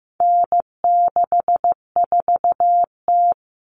Morse code, heard as a keyed tone, about 15 wpm, 700 Hz. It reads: N64T
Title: Common Cut Numbers - 4-Digit 15wpm